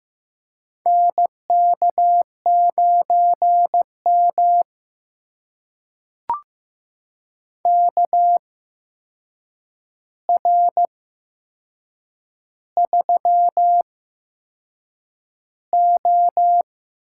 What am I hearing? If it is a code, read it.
Morse code, heard as a keyed tone, about 15 wpm, spaced as for 6 wpm, 700 Hz. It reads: NK9M KR3O
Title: USA Call Signs - 4 Characters - 8x Character Spacing - 15wpm